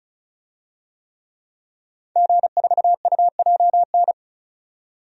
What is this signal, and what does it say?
Morse code, heard as a keyed tone, about 36 wpm, 700 Hz. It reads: G4UJN
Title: DXCC-100 Call Signs - 5 Characters - 35wpm